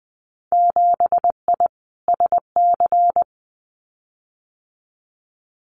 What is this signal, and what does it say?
Morse code, heard as a keyed tone, about 20 wpm, 700 Hz. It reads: 7I SC
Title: Winter Field Day Exchanges Encore 20wpm